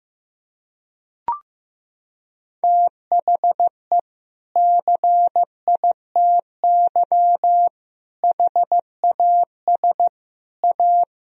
Morse code, heard as a keyed tone, about 15 wpm, 700 Hz. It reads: THE CITY HAS A
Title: Sentences from Top 1000 Words - Review Part 4 - 15wpm